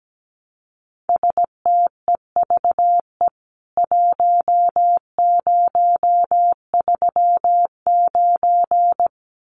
Morse code, heard as a keyed tone, about 17 wpm, 700 Hz. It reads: STEVE 1039
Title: CWT Exchange 17wpm